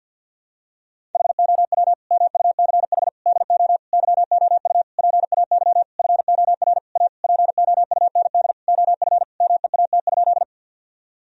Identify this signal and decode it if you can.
Morse code, heard as a keyed tone, about 53 wpm, 700 Hz. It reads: HOW MUCH DO YOU PAY FOR A POUND OF MEAT?
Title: Sentences from Top 1000 Words - Review Part 3 - 50wpm